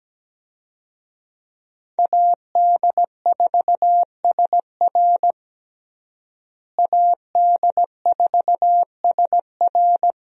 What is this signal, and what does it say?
Morse code, heard as a keyed tone, about 17 wpm, 700 Hz. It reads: AD4SR AD4SR
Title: USA Call Signs - 5 Characters - Repeated 2x - 17wpm